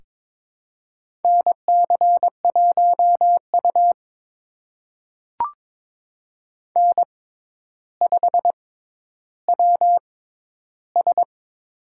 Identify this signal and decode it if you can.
Morse code, heard as a keyed tone, about 22 wpm, spaced as for 11 wpm, 700 Hz. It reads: NC1U N5WS
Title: USA Call Signs - 6x Character Spacing - 22wpm